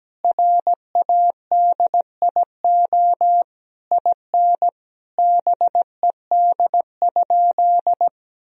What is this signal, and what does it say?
Morse code, heard as a keyed tone, about 17 wpm, 700 Hz. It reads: RADIO IN BED?
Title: Sentences from Top 1000 Words - Review Part 3 - 17wpm